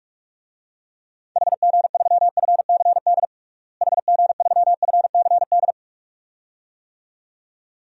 Morse code, heard as a keyed tone, about 41 wpm, 700 Hz. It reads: HG3FCD HG3FCD
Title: DXCC-100 Call Signs - 6 Characters - Repeated 2x - 45wpm